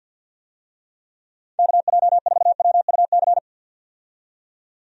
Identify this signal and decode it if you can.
Morse code, heard as a keyed tone, about 51 wpm, 700 Hz. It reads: KJ4WUC
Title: USA Call Signs - 6 Characters - 50wpm